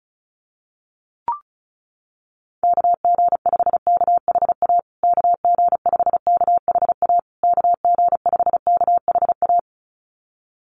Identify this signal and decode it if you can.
Morse code, heard as a keyed tone, about 34 wpm, 700 Hz. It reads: KG5KHA KG5KHA KG5KHA
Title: USA Call Signs Repeated 3x 35wpm